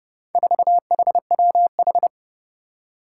Morse code, heard as a keyed tone, about 30 wpm, 700 Hz. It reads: 4HWH
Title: Four Challenging Letter-Number Combinations 30wpm